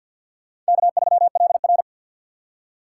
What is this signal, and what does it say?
Morse code, heard as a keyed tone, about 50 wpm, 700 Hz. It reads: K3LR
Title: K1USN SST QSO Practice - 50wpm